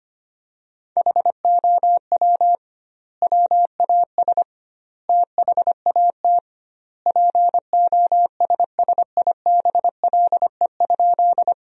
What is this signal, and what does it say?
Morse code, heard as a keyed tone, about 25 wpm, 700 Hz. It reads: HOW WAS THAT POSSIBLE?
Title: 2x Word-Spacing - Sentences from Top 500 Words - 25wpm